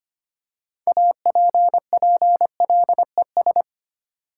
Morse code, heard as a keyed tone, about 25 wpm, 700 Hz. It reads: APPLES
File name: practice / 25wpm / Easy Words with Hints - 25wpm